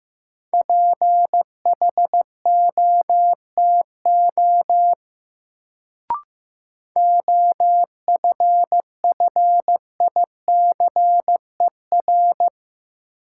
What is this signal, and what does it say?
Morse code, heard as a keyed tone, about 15 wpm, 700 Hz. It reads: PHOTO OFFICER
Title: Sentences from Top 1200 Words - New Word Drill Alt - 15wpm